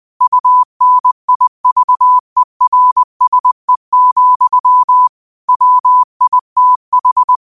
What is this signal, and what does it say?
Morse code, heard as a keyed tone, about 20 wpm, 1000 Hz. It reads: UNIVERSE, WITH
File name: qotdfiles / 20wpm